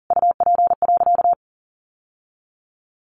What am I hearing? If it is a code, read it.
Morse code, heard as a keyed tone, about 40 wpm, 700 Hz. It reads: UP.
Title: Sentences from Top 700 Words 40wpm